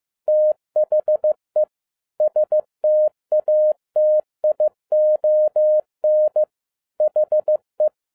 Morse code, heard as a keyed tone, about 15 wpm, 610 Hz. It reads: THE STATION HE
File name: 2011-10-mp3 / n07-A DOT 15wpm.mp3